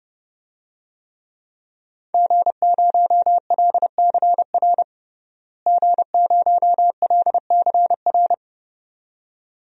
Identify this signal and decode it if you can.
Morse code, heard as a keyed tone, about 30 wpm, 700 Hz. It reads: G0LCR G0LCR
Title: DXCC-100 Call Signs - 5 Characters - Repeated 2x - 30wpm